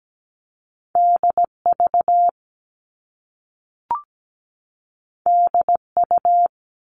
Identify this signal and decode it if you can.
Morse code, heard as a keyed tone, about 17 wpm, 700 Hz. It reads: DV DU
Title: All Cut Numbers - 2-Digit 17wpm